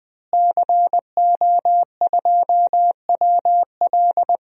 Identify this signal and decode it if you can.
Morse code, heard as a keyed tone, about 21 wpm, 700 Hz. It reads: CO2WL
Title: DXCC-100 Call Signs - 5 Characters - Repeated 3x - 20wpm